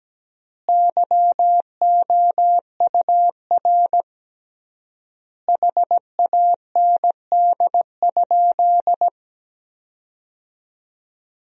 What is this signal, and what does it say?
Morse code, heard as a keyed tone, about 17 wpm, 700 Hz. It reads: YOUR HAND?
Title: 3x Word-Spacing - Sentences from Top 200 Words - 17wpm